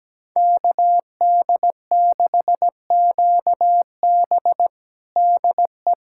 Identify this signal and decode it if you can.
Morse code, heard as a keyed tone, about 17 wpm, 700 Hz. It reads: KD6QB DE
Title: USA Callsign DE Callsign - 5 Characters - 17wpm